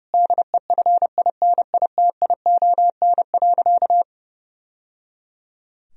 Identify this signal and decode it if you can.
Morse code, heard as a keyed tone, about 30 wpm, 700 Hz. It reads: DEFINITION.
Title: Sentences from Top 1000 Words - Review Part 4 - 30wpm